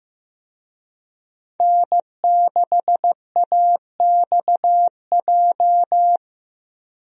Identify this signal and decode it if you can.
Morse code, heard as a keyed tone, about 15 wpm, 700 Hz. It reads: N6AXJ
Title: USA Call Signs - 5 Characters - Repeated 2x - 15wpm